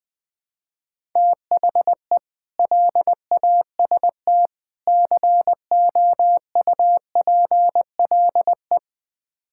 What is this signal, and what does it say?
Morse code, heard as a keyed tone, about 20 wpm, 700 Hz. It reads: THE LAST COUPLE
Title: Build A Phrase - 5-Word Phrases - 20wpm